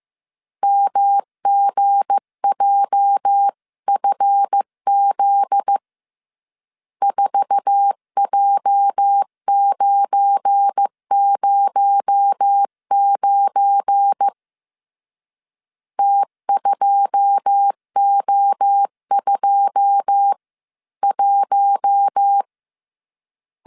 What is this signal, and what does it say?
Morse code, heard as a keyed tone, about 15 wpm, 800 Hz. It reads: MGJFZ 4J909 T2O21
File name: CW / DET / 1 9 1500C S C 14wpm 3245